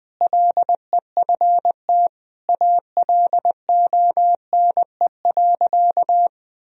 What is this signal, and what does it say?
Morse code, heard as a keyed tone, about 21 wpm, 700 Hz. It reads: LEFT ALONE.
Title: Sentences from Top 1000 Words - Review Part 1 - 20wpm